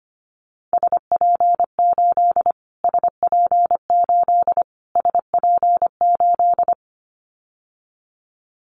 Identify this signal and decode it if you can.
Morse code, heard as a keyed tone, about 25 wpm, 700 Hz. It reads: SP8 SP8 SP8
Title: Callsign Prefixes for Top 25 DXCC - Repeated 3x - 25wpm